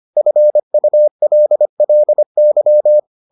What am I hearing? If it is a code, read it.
Morse code, heard as a keyed tone, about 25 wpm, 600 Hz. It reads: FULLY